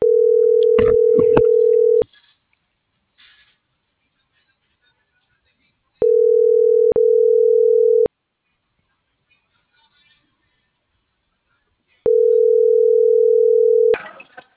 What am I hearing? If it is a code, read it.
Ringback tone (North America), 440 + 480 Hz.